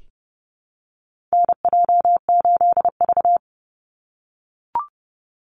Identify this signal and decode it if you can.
Morse code, heard as a keyed tone, about 30 wpm, 700 Hz. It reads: NJ8V
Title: Call Signs - 30wpm